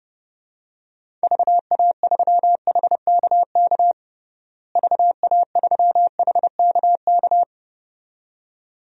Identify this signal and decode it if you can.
Morse code, heard as a keyed tone, about 30 wpm, 700 Hz. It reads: VA3HKK VA3HKK
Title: DXCC-100 Call Signs - 6 Characters - Repeated 2x - 30wpm